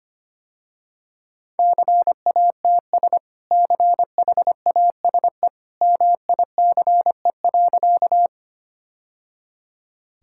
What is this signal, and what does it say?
Morse code, heard as a keyed tone, about 25 wpm, 700 Hz. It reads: CATS CHASE MICE.
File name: Easy Sentences with Hints - 25wpm